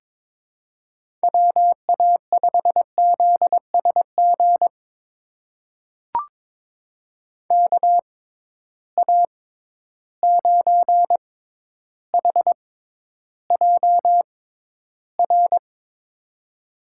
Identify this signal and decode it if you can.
Morse code, heard as a keyed tone, about 23 wpm, spaced as for 11 wpm, 700 Hz. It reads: WA5ZSG KA9HJR
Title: USA Call Signs - 6 Characters - 6x Character Spacing - 22wpm